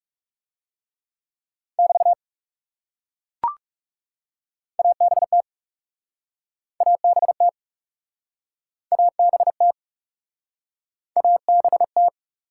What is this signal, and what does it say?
Morse code, heard as a keyed tone, about 38 wpm, 700 Hz. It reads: <BT> ABT ABT ABT ABT